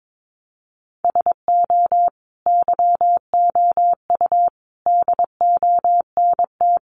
Morse code, heard as a keyed tone, about 22 wpm, 700 Hz. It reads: SO YOU DONT
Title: Sentences from Top 1000 Words 22wpm